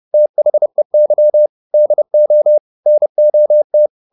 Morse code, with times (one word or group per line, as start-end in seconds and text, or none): 0.1-1.5 THEY
1.7-2.6 DO
2.9-3.9 NOT